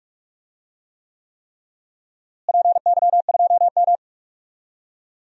Morse code, heard as a keyed tone, about 44 wpm, 700 Hz. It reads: PY2M
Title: DXCC-100 Call Signs - 4 Characters - Repeated 3x - 45wpm